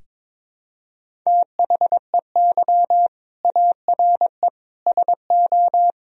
Morse code, heard as a keyed tone, about 22 wpm, 700 Hz. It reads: THEY ARE SO
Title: Sentences from Top 1000 Words - Review Part 1 - 22wpm